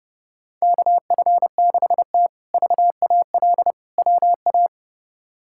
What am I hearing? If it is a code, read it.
Morse code, heard as a keyed tone, about 31 wpm, 700 Hz. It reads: KF6T VAL WA
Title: K1USN SST QSO Practice - 30wpm